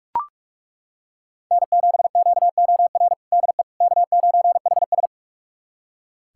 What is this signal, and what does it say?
Morse code, heard as a keyed tone, about 41 wpm, 700 Hz. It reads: N7QOR DE K9HS
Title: USA Callsign DE Callsign 45wpm